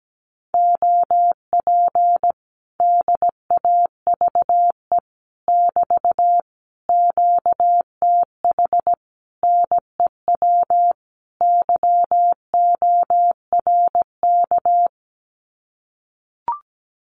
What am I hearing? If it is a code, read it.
Morse code, heard as a keyed tone, about 17 wpm, 700 Hz. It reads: OP DAVE <BT> QTH NEW YORK
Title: Common QSO Phrases Encore 17wpm